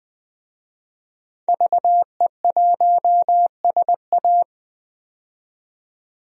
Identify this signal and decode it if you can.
Morse code, heard as a keyed tone, about 20 wpm, 700 Hz. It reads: VE1SA
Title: DXCC-100 Call Signs - 5 Characters - 20wpm